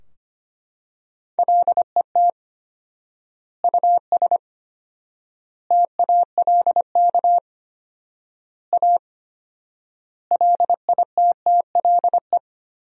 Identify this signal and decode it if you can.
Morse code, heard as a keyed tone, about 25 wpm, 700 Hz. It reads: LET US TALK A LITTLE
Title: 4x Word-Spacing - Sentences from Top 400 Words - 25wpm